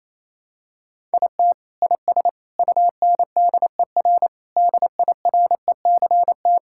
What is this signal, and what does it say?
Morse code, heard as a keyed tone, about 28 wpm, 700 Hz. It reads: IT IS UNDER DIRECT
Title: Sentences from Top 1000 Words - Review Part 5 - 28wpm